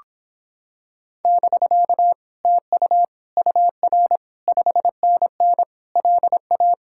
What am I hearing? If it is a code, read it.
Morse code, heard as a keyed tone, about 26 wpm, 700 Hz. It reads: <BK> TU UR 5NN LA